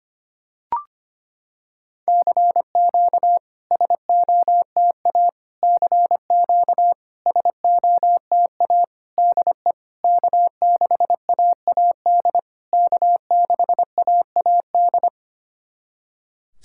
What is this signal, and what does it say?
Morse code, heard as a keyed tone, about 25 wpm, 690 Hz. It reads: CQ SOTA CQ SOTA DE K6AAD K6AAD
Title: SOTA QSO Practice 25wpm